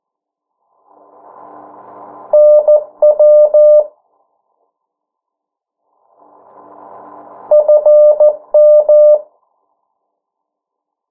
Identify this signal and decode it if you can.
Morse code, heard as a keyed tone, about 14 wpm, 600 Hz. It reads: NW FM
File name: morse mp3 / 09 urok 8